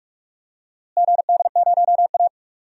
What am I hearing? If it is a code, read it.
Morse code, heard as a keyed tone, about 44 wpm, 700 Hz. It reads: GD0A
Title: DXCC-100 Call Signs - 4 Characters - Repeated 2x - 45wpm